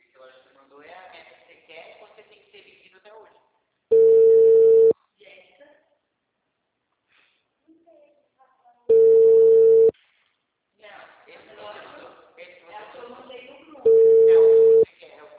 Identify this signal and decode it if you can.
Ringback tone (China, 450 Hz), with speech.
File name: out- -2014- - - .56